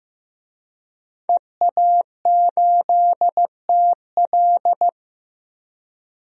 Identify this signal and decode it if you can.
Morse code, heard as a keyed tone, about 15 wpm, 700 Hz. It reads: EA8TL
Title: DXCC-100 Call Signs - 5 Characters - 15wpm